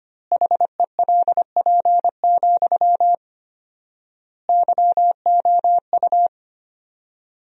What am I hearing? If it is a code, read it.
Morse code, heard as a keyed tone, about 25 wpm, 700 Hz. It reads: HELP, YOU
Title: 4x Word-Spacing - Sentences from Top 300 Words - 25wpm